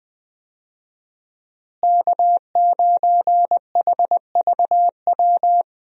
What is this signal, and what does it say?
Morse code, heard as a keyed tone, about 20 wpm, 700 Hz. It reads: K9HVW
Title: USA Call Signs - 5 Characters - Repeated 2x - 20wpm